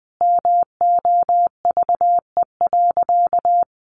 Morse code, heard as a keyed tone, about 20 wpm, 700 Hz. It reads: MOVE.
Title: A Sound Of Thunder 20wpm